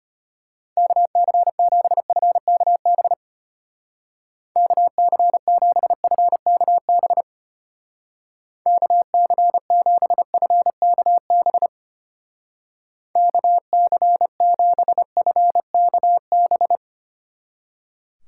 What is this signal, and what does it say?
Morse code, heard as a keyed tone, about 32 wpm, 700 Hz. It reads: KC7FKB KC7FKB KC7FKB KC7FKB